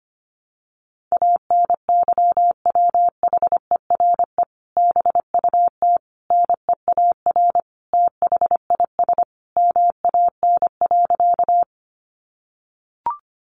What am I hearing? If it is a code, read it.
Morse code, heard as a keyed tone, about 25 wpm, 700 Hz. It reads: ANYWHERE BUT NEAR THIS MAN.
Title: Sentences from Top 1400 Words 25wpm